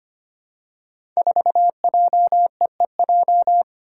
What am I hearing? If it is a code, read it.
Morse code, heard as a keyed tone, about 25 wpm, 700 Hz. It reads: 4JEEJ
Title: Five Challenging Letter-Number Combinations 25wpm